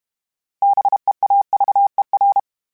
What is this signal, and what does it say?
Morse code, heard as a keyed tone, about 32 wpm, 800 Hz. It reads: BEAVER
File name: fromSMs / FromSMs mp3s / Cities@30